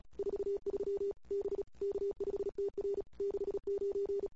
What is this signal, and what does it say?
Morse code, heard as a keyed tone, about 31 wpm, 410 Hz. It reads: 43 B K5TR 69
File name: CW / 2008-ss-cw-k5tr-op-n5rz -21:00